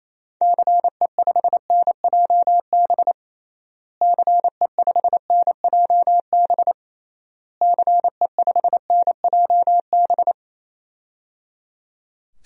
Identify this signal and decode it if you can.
Morse code, heard as a keyed tone, about 28 wpm, 700 Hz. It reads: CE5NJB CE5NJB CE5NJB
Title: DXCC-100 Call Signs - 6 Characters - Repeated 3x - 28wpm